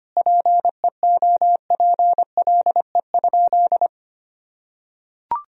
Morse code, heard as a keyed tone, about 25 wpm, 700 Hz. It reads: PEOPLE?
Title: Sentences from Top 1000 Words - Review Part 1 - 25wpm